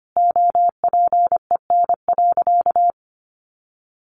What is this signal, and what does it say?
Morse code, heard as a keyed tone, about 25 wpm, 700 Hz. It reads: OPEN.
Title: Sentences from Top 1500 Words 25wpm